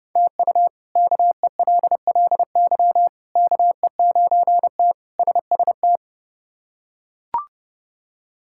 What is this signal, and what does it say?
Morse code, heard as a keyed tone, about 30 wpm, 700 Hz. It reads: TU KELLY KE9T SST
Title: K1USN SST QSO Practice - 30wpm